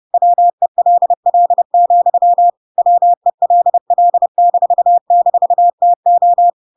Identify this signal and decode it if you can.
Morse code, heard as a keyed tone, about 30 wpm, 700 Hz. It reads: WELL, WELL--TO
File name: Pinocchio / Pinocchio ch1 30 30 ik1hge